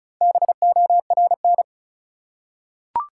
Morse code, heard as a keyed tone, about 34 wpm, 700 Hz. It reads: BORN
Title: Four Letter Words - 35wpm